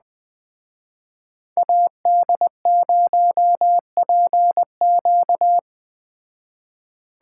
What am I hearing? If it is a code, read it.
Morse code, heard as a keyed tone, about 20 wpm, 700 Hz. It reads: AD0PQ
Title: Speed Racing - USA Call Signs 17wpm